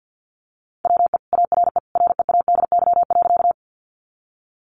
Morse code, heard as a keyed tone, about 48 wpm, 700 Hz. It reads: WE ARE READY.
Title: Sentences from Top 400 Words 50wpm